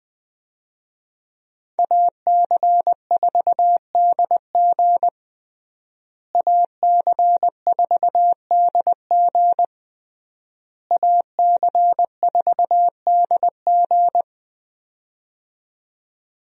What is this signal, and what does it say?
Morse code, heard as a keyed tone, about 20 wpm, 700 Hz. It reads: AC4DG AC4DG AC4DG
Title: USA Call Signs - 5 Characters - Repeated 3x - 20wpm